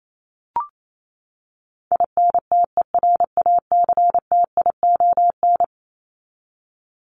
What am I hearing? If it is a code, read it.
Morse code, heard as a keyed tone, about 28 wpm, 700 Hz. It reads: INTERACTION
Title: Top 1251-1500 Words 28wpm